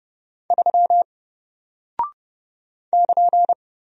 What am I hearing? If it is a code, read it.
Morse code, heard as a keyed tone, about 30 wpm, 700 Hz. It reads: 3 <KN>
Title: Single Character (Letter:Number:Punctuation:Prosign) 30wpm